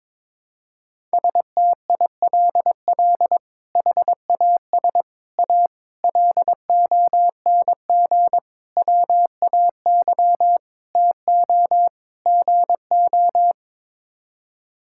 Morse code, heard as a keyed tone, about 22 wpm, 700 Hz. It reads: STILL HAS A LONG WAY TO GO
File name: practice / 22wpm / Sets of 7 Words - 22wpm